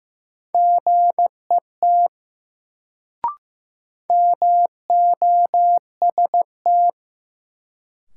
Morse code, heard as a keyed tone, about 15 wpm, 700 Hz. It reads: GET MOST
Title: Sentences from Top 200 Words - New Word Drill - 15wpm